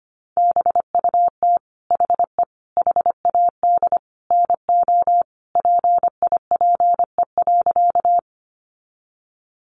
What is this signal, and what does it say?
Morse code, heard as a keyed tone, about 25 wpm, 700 Hz. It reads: BUT HE HAD NO PIPE.
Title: Sentences from Top 1500 Words 25wpm